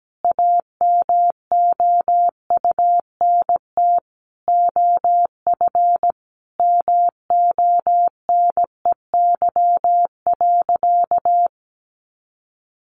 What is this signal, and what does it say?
Morse code, heard as a keyed tone, about 17 wpm, 700 Hz. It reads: AMOUNT OF MONEY.
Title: Sentences from Top 1300 Words 17wpm